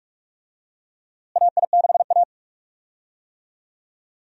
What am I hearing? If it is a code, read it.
Morse code, heard as a keyed tone, about 46 wpm, 700 Hz. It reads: AI6A
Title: USA Call Signs - 4 Characters - Repeated 2x - 45wpm